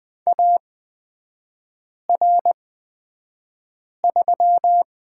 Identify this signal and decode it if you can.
Morse code, heard as a keyed tone, about 20 wpm, spaced as for 5 wpm, 700 Hz. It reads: AR3
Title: 224 Lesson 36e - BT - Three Characters 20wpm